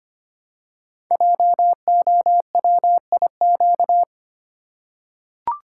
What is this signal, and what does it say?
Morse code, heard as a keyed tone, about 25 wpm, 700 Hz. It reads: JOWIQ
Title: Code Groups - 5 Letters - 25wpm